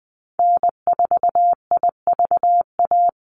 Morse code, heard as a keyed tone, about 20 wpm, 700 Hz. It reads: N4IVA
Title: USA Call Signs Repeated 2x 20wpm